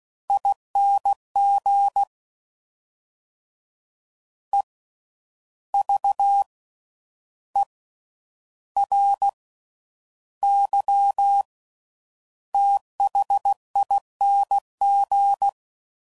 Morse code, heard as a keyed tone, about 16 wpm, spaced as for 9 wpm, 800 Hz. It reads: ING EVERYTHING
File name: Suffixes / Special INGs / Slow With Faster Ings